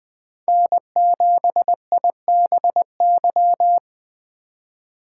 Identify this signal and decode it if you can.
Morse code, heard as a keyed tone, about 24 wpm, 700 Hz. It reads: N7IBY
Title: Speed Racing - USA 1x3 Call Signs 20wpm